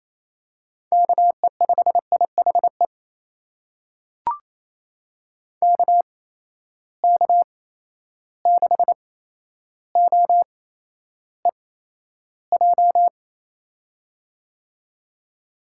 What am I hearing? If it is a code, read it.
Morse code, heard as a keyed tone, about 28 wpm, spaced as for 12 wpm, 700 Hz. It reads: KE5IHE KK6OEJ
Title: USA Call Signs - 6 Characters - 8x Character Spacing - 28wpm